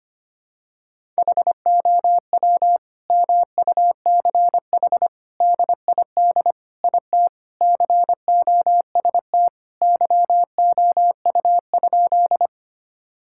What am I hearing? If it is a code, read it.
Morse code, heard as a keyed tone, about 25 wpm, 700 Hz. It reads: HOW MUCH DID IT COST YOU?
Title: Sentences from Top 1000 Words - Review Part 5 - 25wpm